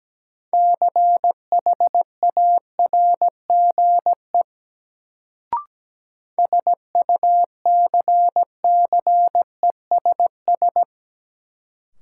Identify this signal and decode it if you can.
Morse code, heard as a keyed tone, about 17 wpm, 700 Hz. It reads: CHARGE SUCCESS
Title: Sentences from Top 900 Words - New Word Drill - 17wpm